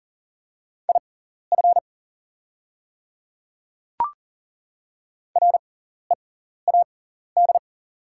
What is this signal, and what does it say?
Morse code, heard as a keyed tone, about 38 wpm, spaced as for 14 wpm, 700 Hz. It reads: IF READ
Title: 6x Character Spacing - Top 500 Words - 40wpm